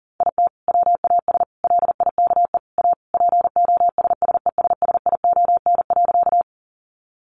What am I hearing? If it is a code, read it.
Morse code, heard as a keyed tone, about 40 wpm, 700 Hz. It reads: IT WAS LIKE A POSSESSION.
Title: Sentences from Top 1400 Words 40wpm